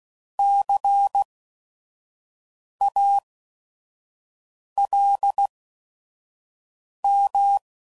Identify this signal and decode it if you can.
Morse code, heard as a keyed tone, about 16 wpm, spaced as for 5 wpm, 800 Hz. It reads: CALM
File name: Prefixes / By First Letter / Cs